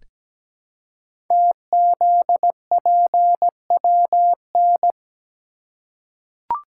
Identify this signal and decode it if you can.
Morse code, heard as a keyed tone, about 17 wpm, 700 Hz. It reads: TZPWN
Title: Code Groups - 5 Letter:Numbers - 17wpm